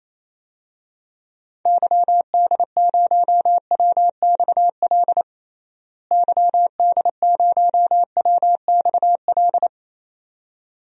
Morse code, heard as a keyed tone, about 28 wpm, 700 Hz. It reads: YD0WXL YD0WXL
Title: DXCC-100 Call Signs - 6 Characters - Repeated 2x - 28wpm